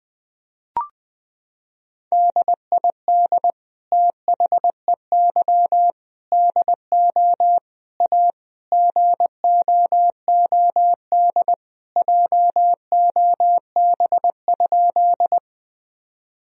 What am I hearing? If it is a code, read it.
Morse code, heard as a keyed tone, about 21 wpm, 700 Hz. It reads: DID THEY DO A GOOD JOB?
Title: Sentences from Top 1000 Words - Review Part 3 - 20wpm